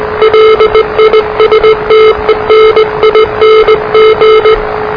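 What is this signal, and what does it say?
Morse code, heard as a keyed tone, about 20 wpm, 440 Hz, with noise. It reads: LISTENING
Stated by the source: Morse challenge from NZ Net, the national amateur radio net of Aotearoa/New Zealand, which meets on 3535.0 kHz.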